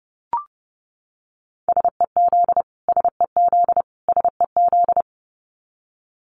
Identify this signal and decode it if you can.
Morse code, heard as a keyed tone, about 28 wpm, 700 Hz. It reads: SEZ SEZ SEZ
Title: Abbreviations Repeated 3x - 30wpm